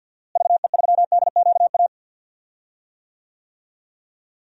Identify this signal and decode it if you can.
Morse code, heard as a keyed tone, about 47 wpm, 700 Hz. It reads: VE3DQA
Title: DXCC-100 Call Signs - 6 Characters - Repeated 3x - 50wpm